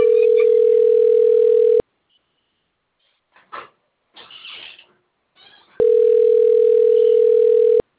Ringback tone (North America), 440 + 480 Hz.
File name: out- - - - - .59633